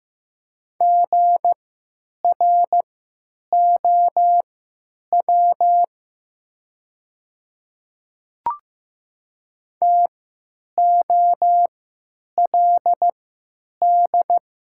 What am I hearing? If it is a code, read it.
Morse code, heard as a keyed tone, about 15 wpm, spaced as for 9 wpm, 700 Hz. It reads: GROW TOLD
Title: 3x Character Spacing - Top 500 Words - 15wpm